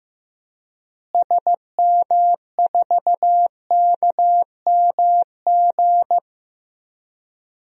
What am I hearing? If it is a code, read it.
Morse code, heard as a keyed tone, about 15 wpm, 700 Hz. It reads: SM4KMG
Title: DXCC-100 Call Signs - 6 Characters - Repeated 3x - 15wpm